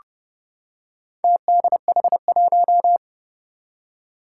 Morse code, heard as a keyed tone, about 30 wpm, 700 Hz. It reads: TDH1
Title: Four Challenging Letter-Number Combinations 30wpm